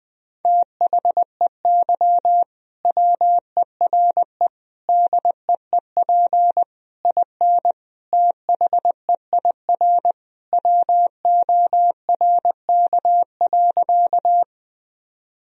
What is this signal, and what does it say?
Morse code, heard as a keyed tone, about 21 wpm, 700 Hz. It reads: THEY WERE DEEP IN THEIR WORK.
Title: Sentences from Top 1000 Words - Review Part 3 - 20wpm